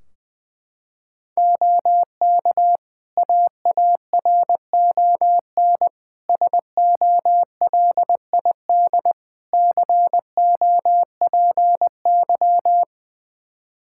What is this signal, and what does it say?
Morse code, heard as a keyed tone, about 20 wpm, 700 Hz. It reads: OK AARON SOLID COPY
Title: K1USN SST QSO Practice 20wpm